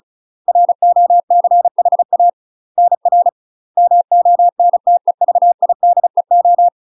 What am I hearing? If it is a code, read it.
Morse code, heard as a keyed tone, about 35 wpm, 700 Hz. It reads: ROCHA NR MONTEVIDEO
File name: UK MASTER QSOs TRAINING/35WPM